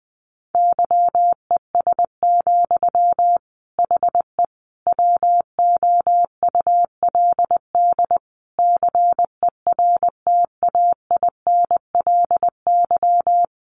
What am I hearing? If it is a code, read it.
Morse code, heard as a keyed tone, about 20 wpm, 700 Hz. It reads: YES, HE WOULD CERTAINLY
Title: Sentences from Top 900 Words 20wpm